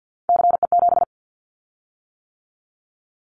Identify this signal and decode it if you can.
Morse code, heard as a keyed tone, about 50 wpm, 700 Hz. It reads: CE7
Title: Callsign Prefixes for Top 100 DXCC - 50wpm